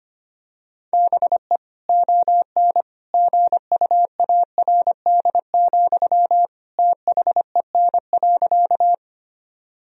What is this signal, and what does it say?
Morse code, heard as a keyed tone, about 25 wpm, 700 Hz. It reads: BE ON GUARD, THEN.
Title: Sentences from Top 1000 Words - Review Part 4 - 25wpm